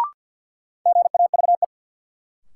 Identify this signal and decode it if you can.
Morse code, heard as a keyed tone, about 53 wpm, 700 Hz. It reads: GAVE
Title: Sentences from Top 500 Words - New Word Drill - 50wpm